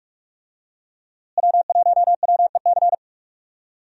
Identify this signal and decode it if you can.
Morse code, heard as a keyed tone, about 44 wpm, 700 Hz. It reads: W1WEC
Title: USA Call Signs - 5 Characters - Repeated 3x - 45wpm